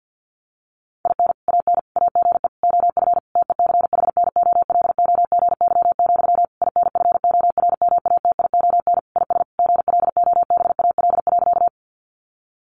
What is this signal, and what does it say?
Morse code, heard as a keyed tone, about 50 wpm, 700 Hz. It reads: IN AN AGE OF TECHNOLOGY, INFORMATION IS GLOBAL.
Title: Sentences from Top 1200 Words 50wpm